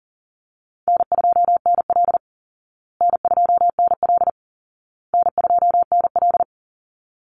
Morse code, heard as a keyed tone, about 40 wpm, 700 Hz. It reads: N2NL N2NL N2NL
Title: Call Signs Repeated 2x - 40wpm